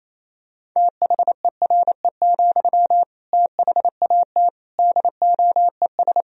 Morse code, heard as a keyed tone, about 28 wpm, 700 Hz. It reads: THERE, THAT DOES
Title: Sentences from Top 200 Words and Abbreviations 28wpm